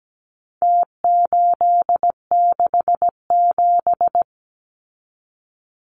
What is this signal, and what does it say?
Morse code, heard as a keyed tone, about 17 wpm, 700 Hz. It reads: T867
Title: Numbers - Contest Style - 4-Digit 17wpm